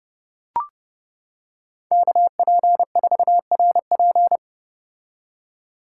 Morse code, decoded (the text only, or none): KP4RP